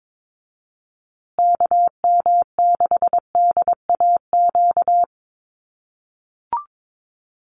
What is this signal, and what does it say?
Morse code, heard as a keyed tone, about 22 wpm, 700 Hz. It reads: KM6DAQ
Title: USA Call Signs Repeated 3x 22wpm